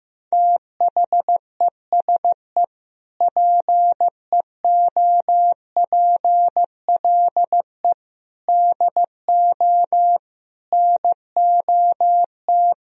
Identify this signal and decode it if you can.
Morse code, heard as a keyed tone, about 15 wpm, 700 Hz. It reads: THESE PEOPLE DO NOT
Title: Sentences from Top 1000 Words - Review Part 1 - 15wpm